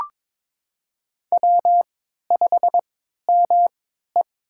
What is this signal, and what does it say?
Morse code, heard as a keyed tone, about 23 wpm, spaced as for 13 wpm, 700 Hz. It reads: W5ME
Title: USA Call Signs - 4 Characters - 3x Character Spacing - 22wpm